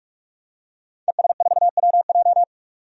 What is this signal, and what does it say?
Morse code, heard as a keyed tone, about 41 wpm, 700 Hz. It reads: ES4WJ
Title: DXCC-100 Call Signs - 5 Characters - 45wpm